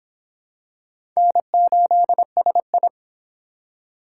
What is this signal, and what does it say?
Morse code, heard as a keyed tone, about 23 wpm, 700 Hz. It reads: N8SI